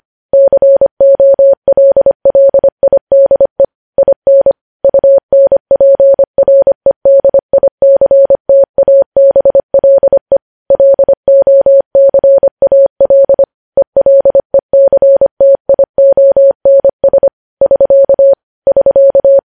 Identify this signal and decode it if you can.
Morse code, heard as a keyed tone, about 25 wpm, 550 Hz. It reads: COLLIDE IN UNPREDICTABLE LOCAL ELECTIONS <SK> <SK>